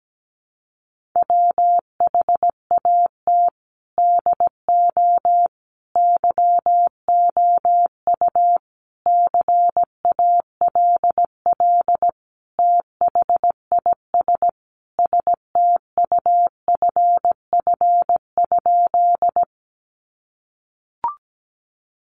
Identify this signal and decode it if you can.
Morse code, heard as a keyed tone, about 17 wpm, 700 Hz. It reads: WHAT DO YOU CALL THIS STUFF?
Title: Sentences from Top 1400 Words 17wpm